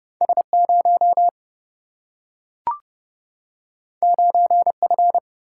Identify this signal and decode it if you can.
Morse code, heard as a keyed tone, about 31 wpm, 700 Hz. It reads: S 0 9 F
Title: Single-Letter Single-Number 30wpm